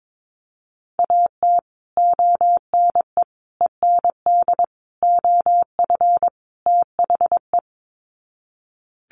Morse code, heard as a keyed tone, about 22 wpm, 700 Hz. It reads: AT ONE END OF THE
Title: Sets of 5 Words Encore 22wpm